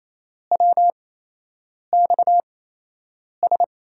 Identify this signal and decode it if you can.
Morse code, heard as a keyed tone, about 28 wpm, spaced as for 8 wpm, 700 Hz. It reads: WXS
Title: USA Call Signs - 5 Characters - 8x Character Spacing - 28wpm